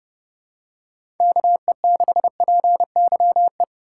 Morse code, heard as a keyed tone, about 30 wpm, 700 Hz. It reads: KE6PYE
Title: USA Call Signs - 6 Characters - 3x Character Spacing - 30wpm